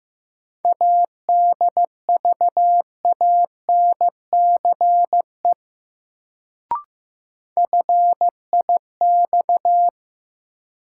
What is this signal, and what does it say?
Morse code, heard as a keyed tone, about 15 wpm, 700 Hz. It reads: ADVANCE FIX
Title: Sentences from Top 1400 Words - New Word Drill - 15wpm